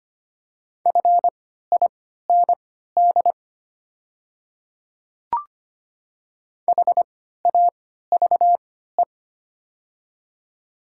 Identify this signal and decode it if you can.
Morse code, heard as a keyed tone, about 25 wpm, spaced as for 14 wpm, 700 Hz. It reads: FIND HAVE
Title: 3x Character Spacing - Top 100 Words - 25wpm